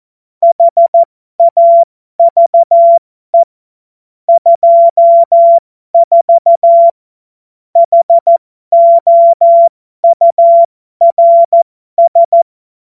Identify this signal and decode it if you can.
Morse code, heard as a keyed tone, about 14 wpm, spaced as for 12 wpm, 680 Hz. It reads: HAVE 24 HOURS
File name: Morsekurs / Witze